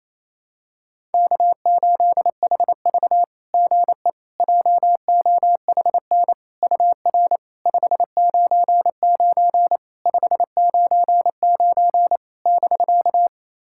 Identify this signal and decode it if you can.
Morse code, heard as a keyed tone, about 27 wpm, 690 Hz. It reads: K8HV GE JOHN UR 599 599 <BK>
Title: SOTA QSO Practice 28wpm